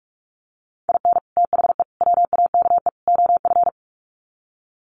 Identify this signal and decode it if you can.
Morse code, heard as a keyed tone, about 44 wpm, 700 Hz. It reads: IN THE WAKE OF
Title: Sets of 4 Words 45wpm